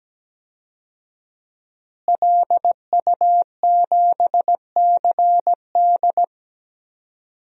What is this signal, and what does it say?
Morse code, heard as a keyed tone, about 17 wpm, 700 Hz. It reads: LU7CD
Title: DXCC-100 Call Signs - 5 Characters - Repeated 3x - 17wpm